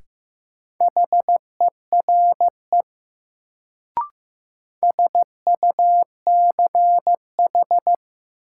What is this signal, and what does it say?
Morse code, heard as a keyed tone, about 15 wpm, 700 Hz. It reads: HERE SUCH
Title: Sentences from Top 200 Words - New Word Drill - 15wpm